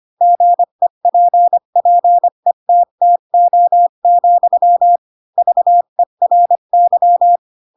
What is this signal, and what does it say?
Morse code, heard as a keyed tone, about 25 wpm, 700 Hz. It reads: GEPPETTO, VERY
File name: Pinocchio / Pinocchio ch7 20 25 ik1hge